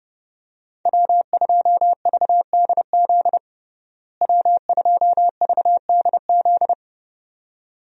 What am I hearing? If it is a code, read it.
Morse code, heard as a keyed tone, about 32 wpm, 700 Hz. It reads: W2VDZ W2VDZ
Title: USA Call Signs - 5 Characters - Repeated 2x - 30wpm